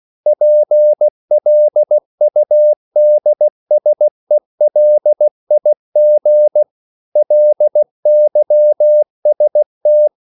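Morse code, heard as a keyed tone, about 16 wpm, 600 Hz. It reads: PLUDSELIG LYST
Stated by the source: Slow (QRS) Morse code at 16 wpm, from CW med Gnister.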